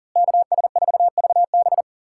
Morse code, heard as a keyed tone, about 40 wpm, 700 Hz. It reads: KS4VB
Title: USA Callsign DE Callsign 40wpm